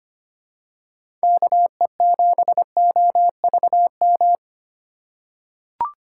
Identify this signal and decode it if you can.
Morse code, heard as a keyed tone, about 25 wpm, 700 Hz. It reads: KE7OVM
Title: USA Call Signs - 6 Characters - 4x Character Spacing - 25wpm